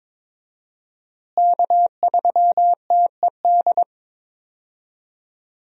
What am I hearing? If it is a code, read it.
Morse code, heard as a keyed tone, about 22 wpm, 700 Hz. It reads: K3TED
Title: USA Call Signs - 5 Characters - Repeated 3x - 22wpm